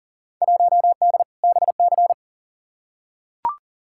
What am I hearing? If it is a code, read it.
Morse code, heard as a keyed tone, about 40 wpm, 700 Hz. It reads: 1D BC
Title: Field Day Exchanges - 40wpm